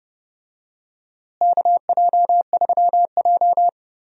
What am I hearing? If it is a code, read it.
Morse code, heard as a keyed tone, about 30 wpm, 700 Hz. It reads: KJ3J
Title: USA Call Signs - 4 Characters - Repeated 3x - 30wpm